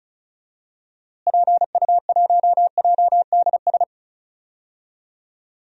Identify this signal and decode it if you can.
Morse code, heard as a keyed tone, about 34 wpm, 700 Hz. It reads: PU1JDS
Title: DXCC-100 Call Signs - 6 Characters - Repeated 2x - 35wpm